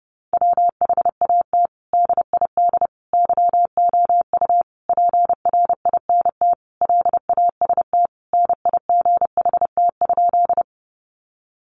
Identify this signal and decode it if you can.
Morse code, heard as a keyed tone, about 30 wpm, 700 Hz. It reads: WHAT DID YOU PRINT LAST NIGHT?
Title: Sentences from Top 1400 Words 30wpm